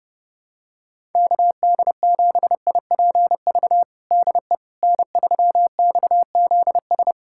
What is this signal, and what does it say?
Morse code, heard as a keyed tone, about 30 wpm, 700 Hz. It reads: KD7IPV DE N3XZS
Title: USA Callsign DE Callsign 30wpm